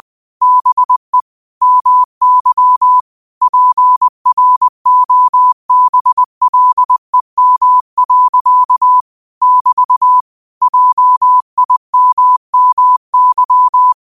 Morse code, heard as a keyed tone, about 20 wpm, 1000 Hz. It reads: BE MY PROBLEM. <BT> JIMMY
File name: qotdfiles / 20wpm